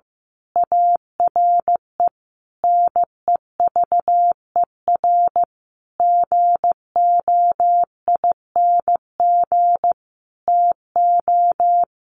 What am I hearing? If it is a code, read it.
Morse code, heard as a keyed tone, about 15 wpm, 700 Hz. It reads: ARE NEVER GOING TO
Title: A Sound Of Thunder 15wpm